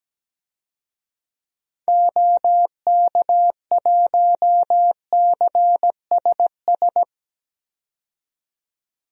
Morse code, heard as a keyed tone, about 17 wpm, 700 Hz. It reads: OK1CSS
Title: DXCC-100 Call Signs - 6 Characters - 17wpm